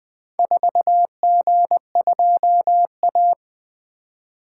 Morse code, heard as a keyed tone, about 20 wpm, 700 Hz. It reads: 4G2A
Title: Four Letter-Number Combinations - 20wpm